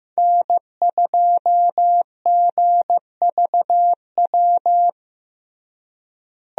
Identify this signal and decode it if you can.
Morse code, heard as a keyed tone, about 16 wpm, 700 Hz. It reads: N2GVW
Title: USA Call Signs - 5 Characters - Repeated 3x - 15wpm